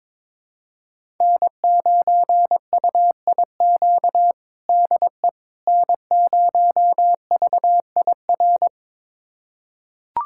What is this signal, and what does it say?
Morse code, heard as a keyed tone, about 22 wpm, 700 Hz. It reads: N9UIQ DE N0VIR
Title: USA Callsign DE Callsign - 5 Characters - 22wpm